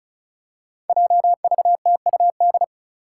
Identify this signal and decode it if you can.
Morse code, heard as a keyed tone, about 36 wpm, 700 Hz. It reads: JVTUD
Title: Five Challenging Letter-Number Combinations 35wpm